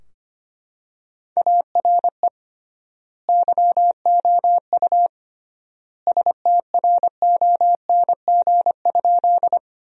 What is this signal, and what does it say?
Morse code, heard as a keyed tone, about 25 wpm, 700 Hz. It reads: ARE YOU STRONG?
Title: 3x Word-Spacing - Sentences from Top 500 Words - 25wpm